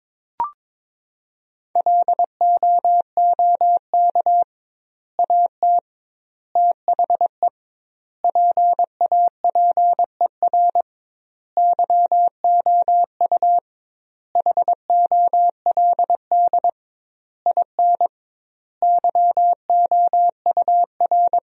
Morse code, heard as a keyed tone, about 23 wpm, 700 Hz. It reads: LOOK AT THE PAPER YOU HOLD IN YOUR
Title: 2x Word-Spacing - Sentences from Top 400 Words - 22wpm